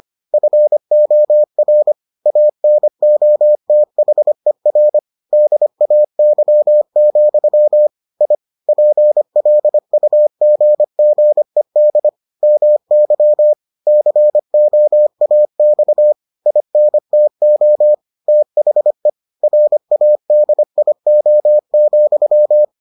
Morse code, heard as a keyed tone, about 25 wpm, 600 Hz. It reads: FOR ANOTHER DAY, I PLUGGED MY COAX INTO THE RADIO,